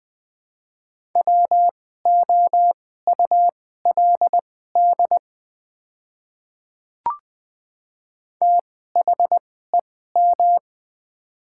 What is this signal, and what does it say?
Morse code, heard as a keyed tone, about 20 wpm, spaced as for 14 wpm, 700 Hz. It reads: WOULD THEM
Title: 2x Character Spacing - Top 100 Words - 20wpm